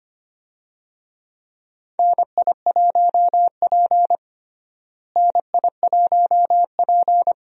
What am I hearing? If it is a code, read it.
Morse code, heard as a keyed tone, about 25 wpm, 700 Hz. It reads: NI1P NI1P
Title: USA Call Signs - 4 Characters - Repeated 3x - 25wpm